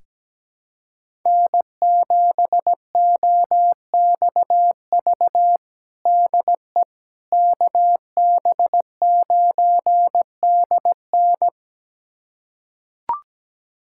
Morse code, heard as a keyed tone, about 17 wpm, 700 Hz. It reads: N7OXV DE KB9DN
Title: USA Callsign DE Callsign - 5 Characters - 17wpm